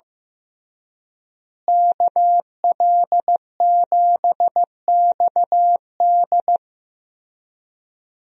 Morse code, heard as a keyed tone, about 15 wpm, 700 Hz. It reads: KL7XD
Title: DXCC-100 Call Signs - 5 Characters - Repeated 3x - 15wpm